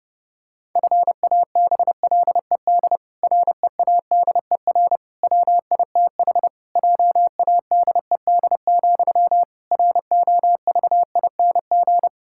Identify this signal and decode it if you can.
Morse code, heard as a keyed tone, about 30 wpm, 700 Hz. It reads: FABLED READER WITH JADED, ROVING
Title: Pangram Challenge v2 - 30wpm